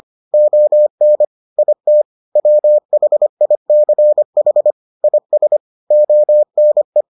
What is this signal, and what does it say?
Morse code, heard as a keyed tone, about 25 wpm, 600 Hz. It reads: ON IT WHICH IS ONE